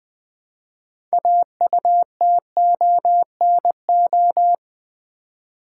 Morse code, heard as a keyed tone, about 20 wpm, 700 Hz. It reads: AUTONO
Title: Build A Word - 2 Letters At A Time - 8-Letter Words - 20wpm